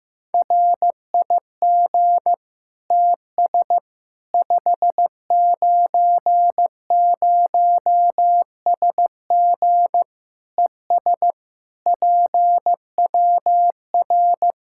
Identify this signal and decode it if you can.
Morse code, heard as a keyed tone, about 15 wpm, 700 Hz. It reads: RIG TS 590SG ES PWR
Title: Practice QSO - Exchange 2 - Info - 15wpm